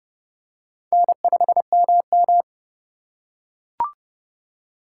Morse code, heard as a keyed tone, about 27 wpm, 700 Hz. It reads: N5MM